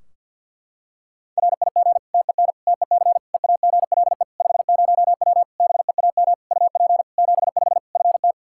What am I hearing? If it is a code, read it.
Morse code, heard as a keyed tone, about 52 wpm, 700 Hz. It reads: RIG TEN TEC EAGLE 50W BEAM UP 75 FT
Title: Practice QSO - Exchange 2 - Info - 50wpm